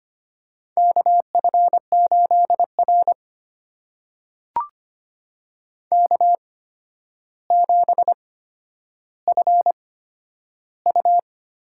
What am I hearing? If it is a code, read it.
Morse code, heard as a keyed tone, about 25 wpm, spaced as for 11 wpm, 700 Hz. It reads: KF8R K7FU
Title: USA Call Signs - 4 Characters - 8x Character Spacing - 25wpm